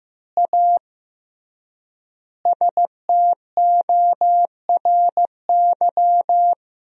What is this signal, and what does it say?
Morse code, heard as a keyed tone, about 15 wpm, 700 Hz. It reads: A STORY
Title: 3x Word-Spacing - Sentences from Top 500 Words - 15wpm